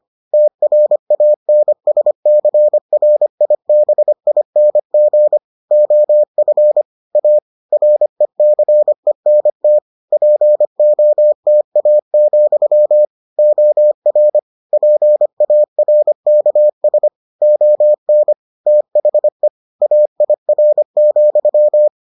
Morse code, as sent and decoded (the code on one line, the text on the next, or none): - .-. .- -. ... -.-. .-. .. -... .. -. --. / --- ..-. / .- / .-. -.-. -. - / .--. --- - .- --..-- / --- .-. / .--. .- .-. -.- ... / --- -. / - .... / .- .. .-. --..--
TRANSCRIBING OF A RECENT POTA, OR PARKS ON THE AIR,